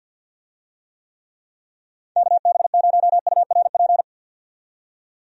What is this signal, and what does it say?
Morse code, heard as a keyed tone, about 47 wpm, 700 Hz. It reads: KB0URP
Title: USA Call Signs - 6 Characters - 50wpm